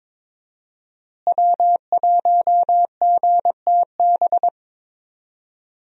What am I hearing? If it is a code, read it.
Morse code, heard as a keyed tone, about 22 wpm, 700 Hz. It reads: W1GTB
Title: USA Call Signs - 5 Characters - Repeated 2x - 22wpm